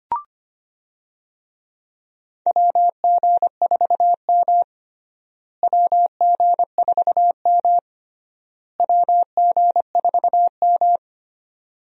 Morse code, heard as a keyed tone, about 25 wpm, 700 Hz. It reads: WG4M WG4M WG4M
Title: USA Call Signs - 4 Characters - Repeated 3x - 25wpm